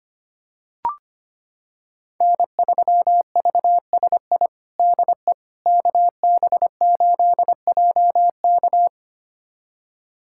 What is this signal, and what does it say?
Morse code, heard as a keyed tone, about 25 wpm, 700 Hz. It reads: N3VSI DE KB8JK
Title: USA Callsign DE Callsign - 5 Characters - 25wpm